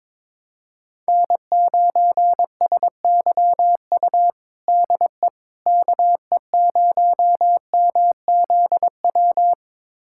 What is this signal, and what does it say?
Morse code, heard as a keyed tone, about 22 wpm, 700 Hz. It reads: N9SYU DE KE0MZW
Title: USA Callsign DE Callsign 22wpm